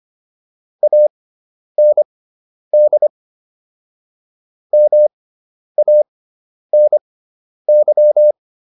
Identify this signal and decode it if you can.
Morse code, heard as a keyed tone, about 25 wpm, spaced as for 10 wpm, 600 Hz. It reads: AND MANY